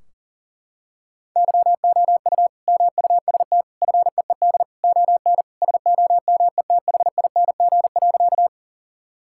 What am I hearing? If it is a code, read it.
Morse code, heard as a keyed tone, about 40 wpm, 700 Hz. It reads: YOU MUST FEED ON SOMETHING.
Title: Sentences from Top 1600 Words 40wpm